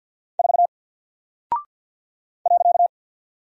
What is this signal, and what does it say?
Morse code, heard as a keyed tone, about 49 wpm, 700 Hz. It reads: 4 .
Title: Single Character (Letter:Number:Punctuation:Prosign) Alt 50wpm